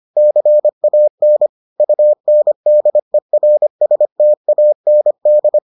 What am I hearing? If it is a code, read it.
Morse code, heard as a keyed tone, about 25 wpm, 600 Hz. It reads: CAN UNDERSTAND